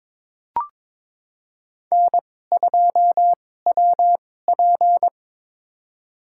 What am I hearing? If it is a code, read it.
Morse code, heard as a keyed tone, about 22 wpm, spaced as for 16 wpm, 700 Hz. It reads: N2WP
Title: USA Call Signs - 4 Characters - 2x Character Spacing - 22wpm